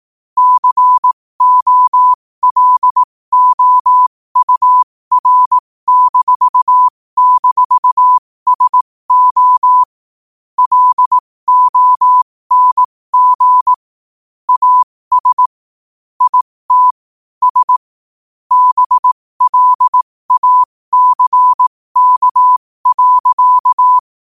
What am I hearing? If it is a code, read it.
Morse code, heard as a keyed tone, about 18 wpm, spaced as for 15 wpm, 1000 Hz. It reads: COLOUR--SO LONG AS IT S BLACK.
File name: qotdfiles / 15wpm